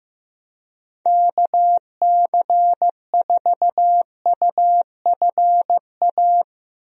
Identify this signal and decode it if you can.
Morse code, heard as a keyed tone, about 15 wpm, 700 Hz. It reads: KC4UFA
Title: USA Call Signs - 6 Characters - Repeated 2x - 15wpm